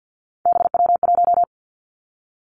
Morse code, heard as a keyed tone, about 50 wpm, 700 Hz. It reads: BW1
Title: Three Challenging Letter-Number Combinations 50wpm